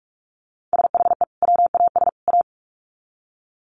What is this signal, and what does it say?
Morse code, heard as a keyed tone, about 44 wpm, 700 Hz. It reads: SHE WAS A
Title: Sets of 3 Words 45wpm